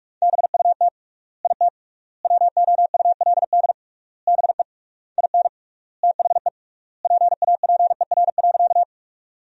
Morse code, heard as a keyed tone, about 47 wpm, 700 Hz. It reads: BUT IT WOULD BE IN THE PAPER.
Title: 3x Word-Spacing - Sentences from Top 300 Words - 45wpm